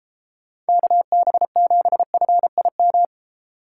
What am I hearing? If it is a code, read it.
Morse code, heard as a keyed tone, about 34 wpm, 700 Hz. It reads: KB7FIM